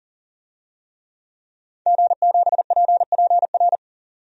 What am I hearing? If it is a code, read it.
Morse code, heard as a keyed tone, about 40 wpm, 700 Hz. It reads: G7PPR
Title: DXCC-100 Call Signs - 5 Characters - 40wpm